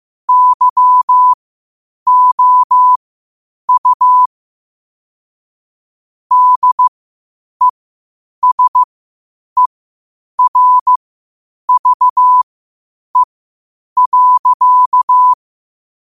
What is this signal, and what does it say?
Morse code, heard as a keyed tone, about 15 wpm, spaced as for 8 wpm, 1000 Hz. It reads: YOU DESERVE.
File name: qotdfiles / 8wpm